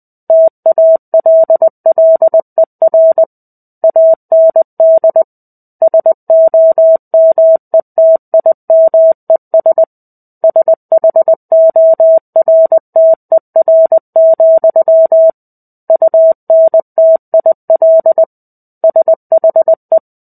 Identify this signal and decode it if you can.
Morse code, heard as a keyed tone, about 20 wpm, 650 Hz. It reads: TALLER AND SOMETIMES SHORTER, UNTIL SHE
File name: 05-Chapter-05 20 WPM